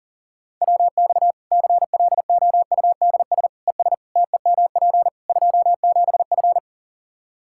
Morse code, heard as a keyed tone, about 40 wpm, 700 Hz. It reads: WX CLOUDS ES TEMP 27F
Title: Practice QSO - Exchange 2 - Info - 40wpm